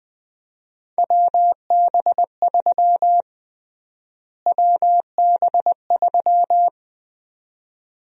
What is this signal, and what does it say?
Morse code, heard as a keyed tone, about 20 wpm, 700 Hz. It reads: WB3 WB3
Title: USA Call Sign Prefixes - Repeated 2x - 20wpm